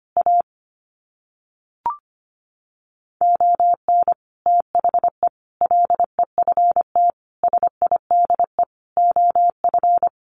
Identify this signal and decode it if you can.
Morse code, heard as a keyed tone, about 25 wpm, 700 Hz. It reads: A ON THE LEFT SIDE OF
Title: Sets of 5 Words Encore2 25wpm